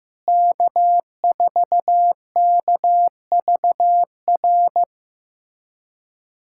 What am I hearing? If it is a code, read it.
Morse code, heard as a keyed tone, about 15 wpm, 700 Hz. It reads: K4KVR
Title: USA Callsign DE Callsign - 5 Characters - 15wpm